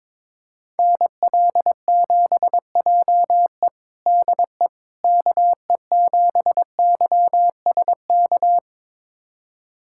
Morse code, heard as a keyed tone, about 21 wpm, 700 Hz. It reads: NL7JE DE KE7YSK
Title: USA Callsign DE Callsign 22wpm